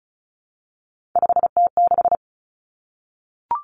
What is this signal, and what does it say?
Morse code, heard as a keyed tone, about 36 wpm, 700 Hz. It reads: 5T6
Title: Three Challenging Letter-Number Combinations 35wpm